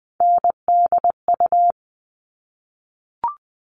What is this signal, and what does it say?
Morse code, heard as a keyed tone, about 20 wpm, 700 Hz. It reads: NDU
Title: All Cut Numbers - 3-Digit 20wpm